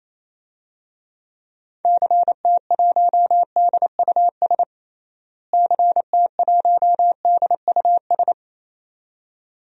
Morse code, heard as a keyed tone, about 28 wpm, 700 Hz. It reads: CT1DUS CT1DUS
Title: DXCC-100 Call Signs - 6 Characters - Repeated 2x - 28wpm